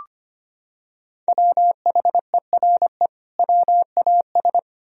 Morse code, heard as a keyed tone, about 25 wpm, 700 Hz. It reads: WHERE WAS
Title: Sentences from Top 1000 Words - Review Part 3 - 25wpm